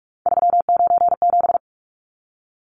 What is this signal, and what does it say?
Morse code, heard as a keyed tone, about 44 wpm, 700 Hz. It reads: 397
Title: Numbers - 3-Digit 45wpm